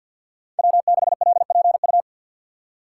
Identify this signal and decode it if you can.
Morse code, heard as a keyed tone, about 51 wpm, 700 Hz. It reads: W6LPU
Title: USA Call Signs - 2x Character Spacing - 50wpm